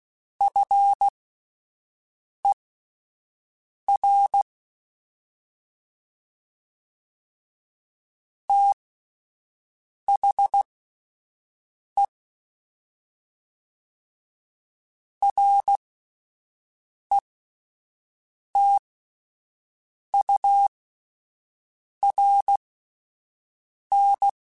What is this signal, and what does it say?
Morse code, heard as a keyed tone, about 16 wpm, spaced as for 5 wpm, 800 Hz. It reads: FER THE RETURN
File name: QSOs Spaced